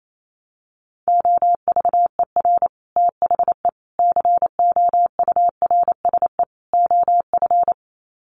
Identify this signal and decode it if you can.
Morse code, heard as a keyed tone, about 28 wpm, 700 Hz. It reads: OVER THE COURSE OF
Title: Sets of 4 Words 28wpm